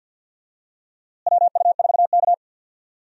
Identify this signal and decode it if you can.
Morse code, heard as a keyed tone, about 52 wpm, 700 Hz. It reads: WU4K
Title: USA Call Signs - 4 Characters - Repeated 3x - 50wpm